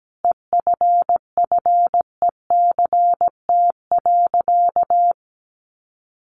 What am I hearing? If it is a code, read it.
Morse code, heard as a keyed tone, about 17 wpm, 700 Hz. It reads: EFFECT.
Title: Sentences from Top 1000 Words 17wpm